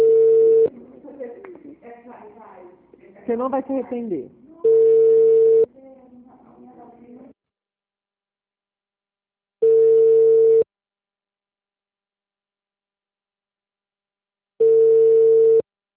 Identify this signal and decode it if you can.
Ringback tone (China, 450 Hz), with speech.
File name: out- -2004- - - .128